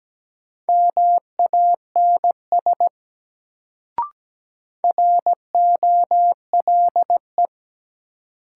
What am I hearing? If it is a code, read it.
Morse code, heard as a keyed tone, about 17 wpm, 700 Hz. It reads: MANS ROLE
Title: Sentences from Top 1000 Words - Review Part 6 - New Word Drill - 17wpm